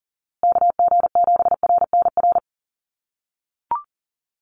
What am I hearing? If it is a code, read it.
Morse code, heard as a keyed tone, about 39 wpm, 700 Hz. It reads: KG7RNR
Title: USA Call Signs Repeated 2x 40wpm